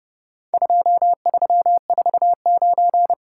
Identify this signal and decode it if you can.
Morse code, heard as a keyed tone, about 30 wpm, 700 Hz. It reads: 2349
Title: Worked All Europe DX Contest - QTC Exchange - 30wpm